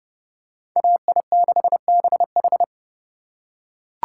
Morse code, heard as a keyed tone, about 28 wpm, 700 Hz. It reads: AI6BH